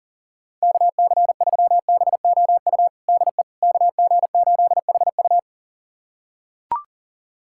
Morse code, heard as a keyed tone, about 37 wpm, 700 Hz. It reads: KC3BOU DE KG8HU
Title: USA Callsign DE Callsign 40wpm